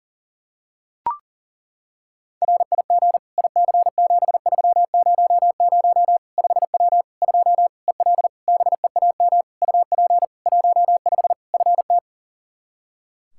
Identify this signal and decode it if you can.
Morse code, heard as a keyed tone, about 40 wpm, 700 Hz. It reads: RIG IC7300 5W 2 EL BEAM UP 15 FT
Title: K1USN SST QSO Practice 40wpm